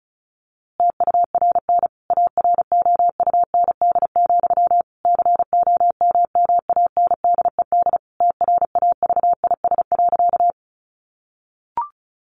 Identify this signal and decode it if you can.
Morse code, heard as a keyed tone, about 34 wpm, 700 Hz. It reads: TURN AROUND, COMMANDED TRAVIS.
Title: A Sound Of Thunder 35wpm